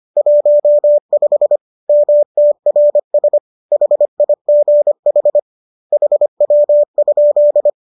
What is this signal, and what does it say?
Morse code, heard as a keyed tone, about 25 wpm, 590 Hz. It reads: 15 MTRS HIGH HW?
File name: qso219 25